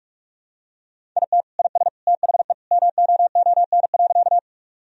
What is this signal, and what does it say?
Morse code, heard as a keyed tone, about 46 wpm, 700 Hz. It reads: IT IS THE MOON.
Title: Sentences from Top 1000 Words - Review Part 3 - 45wpm